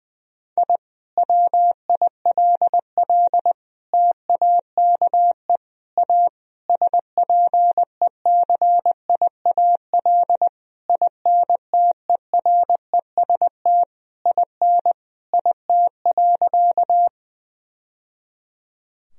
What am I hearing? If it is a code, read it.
Morse code, heard as a keyed tone, about 20 wpm, 700 Hz. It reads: I WILL TAKE A SPECIAL INTEREST IN IT.
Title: Sentences from Top 1000 Words - Review Part 3 - 20wpm